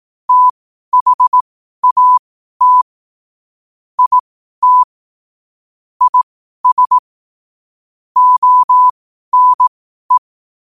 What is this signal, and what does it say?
Morse code, heard as a keyed tone, about 18 wpm, spaced as for 12 wpm, 1000 Hz. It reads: THAT IT IS ONE
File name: qotdfiles / 12wpm